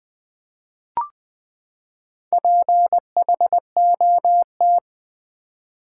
Morse code, heard as a keyed tone, about 20 wpm, 700 Hz. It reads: PHOT
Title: Build A Word - 4 Letters At A Time - 12-Letter Words - 20wpm